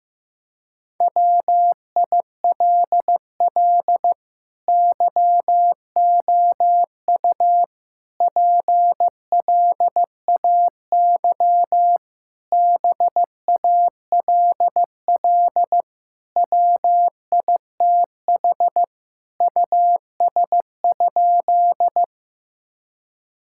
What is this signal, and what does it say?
Morse code, heard as a keyed tone, about 15 wpm, 700 Hz. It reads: WILL YOU PLAY BALL WITH US?
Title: Sentences from Top 1000 Words - Review Part 6 - 15wpm